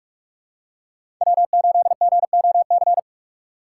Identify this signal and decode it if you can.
Morse code, heard as a keyed tone, about 46 wpm, 700 Hz. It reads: W8GOC
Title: USA Call Signs - 5 Characters - 3x Character Spacing - 45wpm